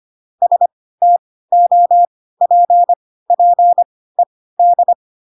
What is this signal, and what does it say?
Morse code, heard as a keyed tone, about 25 wpm, spaced as for 16 wpm, 700 Hz. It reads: STOPPED
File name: Pinocchio / Pinocchio ch3 10 25 ik1hge